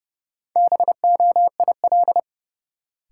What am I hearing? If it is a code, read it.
Morse code, heard as a keyed tone, about 30 wpm, 700 Hz. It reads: BOIL
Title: Sets of 5 Words Encore - New Word Drill - 30wpm